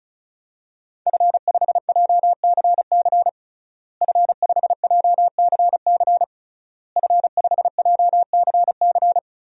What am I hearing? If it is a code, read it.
Morse code, heard as a keyed tone, about 36 wpm, 700 Hz. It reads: F5JCC F5JCC F5JCC
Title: DXCC-100 Call Signs - 5 Characters - Repeated 3x - 35wpm